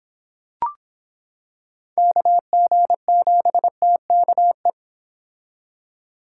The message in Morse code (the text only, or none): KG7TKE